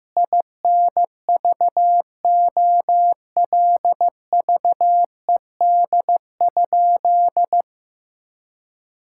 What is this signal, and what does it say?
Morse code, heard as a keyed tone, about 15 wpm, 700 Hz. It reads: INVOLVED?
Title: Sentences from Top 1000 Words - Review Part 5 - 15wpm